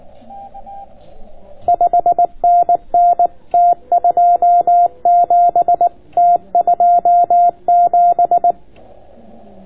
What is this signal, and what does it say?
Morse code, heard as a keyed tone, about 19 wpm, 700 Hz. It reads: 5NN T27 T27